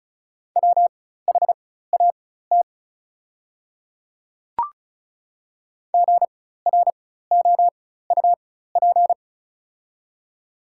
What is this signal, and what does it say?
Morse code, heard as a keyed tone, about 36 wpm, spaced as for 16 wpm, 700 Hz. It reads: WHAT GROUP
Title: 4x Character Spacing - Top 300 Words - 35wpm